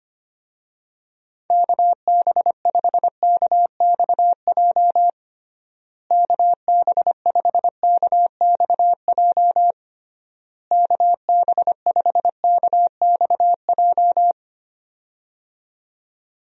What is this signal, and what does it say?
Morse code, heard as a keyed tone, about 25 wpm, 700 Hz. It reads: KB5KXJ KB5KXJ KB5KXJ
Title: USA Call Signs - 6 Characters - Repeated 3x - 25wpm